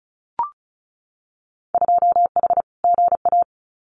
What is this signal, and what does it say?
Morse code, heard as a keyed tone, about 36 wpm, 700 Hz. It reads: 2H GA
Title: Winter Field Day Exchanges 35wpm